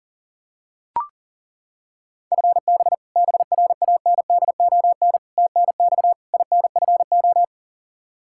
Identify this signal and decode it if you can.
Morse code, heard as a keyed tone, about 37 wpm, 700 Hz. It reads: FB BRANDON TNX INFO
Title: Practice QSO - Exchange 3 - More Info - 40wpm